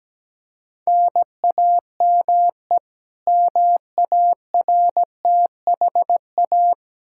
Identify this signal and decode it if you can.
Morse code, heard as a keyed tone, about 17 wpm, 700 Hz. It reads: NAME MARTHA
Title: Practice QSO - Exchange 1 - Beginning - 17wpm